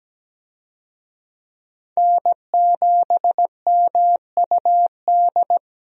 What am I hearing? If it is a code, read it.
Morse code, heard as a keyed tone, about 17 wpm, 700 Hz. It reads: N7MUD
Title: USA Call Signs - 5 Characters - Repeated 2x - 17wpm